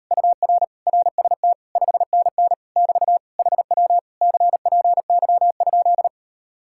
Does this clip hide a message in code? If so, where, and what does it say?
Yes: Morse code from 0.1 to 6.1 s, heard as a keyed tone, about 38 wpm, 700 Hz. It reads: UR RST 5NN <BT> HW CPY?